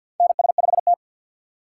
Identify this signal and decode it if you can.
Morse code, heard as a keyed tone, about 52 wpm, 700 Hz. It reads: NS5T
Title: USA Call Signs - 4x Character Spacing - 50wpm